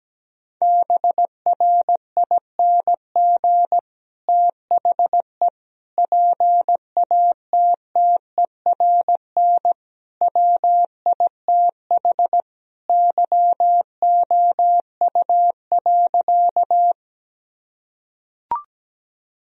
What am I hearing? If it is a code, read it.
Morse code, heard as a keyed tone, about 17 wpm, 700 Hz. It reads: BRING THE PATTERN WITH YOU.
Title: Sentences from Top 1000 Words - Review Part 2 - 17wpm